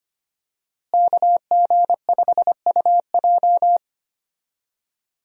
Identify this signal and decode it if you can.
Morse code, heard as a keyed tone, about 25 wpm, 700 Hz. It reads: KG5UJ
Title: USA Call Signs - 5 Characters - Repeated 3x - 25wpm